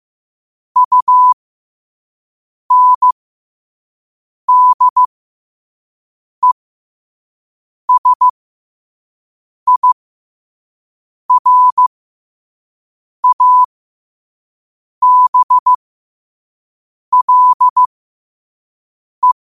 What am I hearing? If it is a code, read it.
Morse code, heard as a keyed tone, about 15 wpm, spaced as for 5 wpm, 1000 Hz. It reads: UNDESIRABLE
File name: qotdfiles / 5wpm